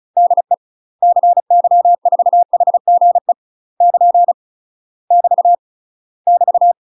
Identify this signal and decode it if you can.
Morse code, heard as a keyed tone, about 36 wpm, 700 Hz. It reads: DE CY4HGE <KN> <BT> <BT>
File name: UK MASTER QSOs TRAINING / 35WPM / UK MASTER QSOs 0002